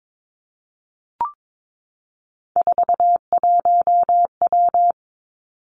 Morse code, heard as a keyed tone, about 22 wpm, 700 Hz. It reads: 41W
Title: Three Challenging Letter-Number Combinations 22wpm